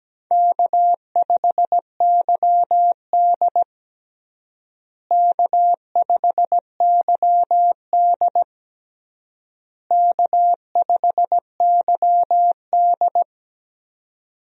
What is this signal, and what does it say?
Morse code, heard as a keyed tone, about 17 wpm, 700 Hz. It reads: K5YD K5YD K5YD
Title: USA Call Signs - 4 Characters - Repeated 3x - 17wpm